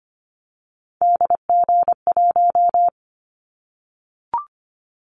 Morse code, heard as a keyed tone, about 25 wpm, 700 Hz. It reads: DG1
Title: Callsign Prefixes for Top 25 DXCC - 25wpm